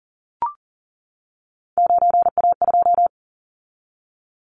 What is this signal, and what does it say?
Morse code, heard as a keyed tone, about 38 wpm, 700 Hz. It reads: 9A2
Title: Callsign Prefixes for Top 50 DXCC - 40wpm